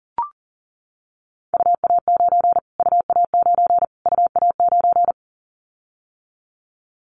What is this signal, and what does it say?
Morse code, heard as a keyed tone, about 40 wpm, 700 Hz. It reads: UA9 UA9 UA9
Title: Callsign Prefixes for Top 25 DXCC - Repeated 3x - 40wpm